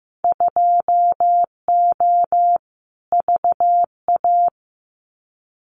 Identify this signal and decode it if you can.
Morse code, heard as a keyed tone, about 15 wpm, 700 Hz. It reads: 2O VA
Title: Winter Field Day Exchanges 15wpm